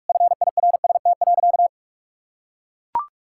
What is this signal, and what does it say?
Morse code, heard as a keyed tone, about 43 wpm, 700 Hz. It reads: FIRST.
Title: Sentences from Top 1000 Words - Review Part 6 - 45wpm